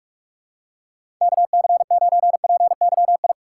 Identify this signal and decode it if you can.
Morse code, heard as a keyed tone, about 46 wpm, 700 Hz. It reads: KC9PYI
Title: USA Call Signs - 4x Character Spacing - 45wpm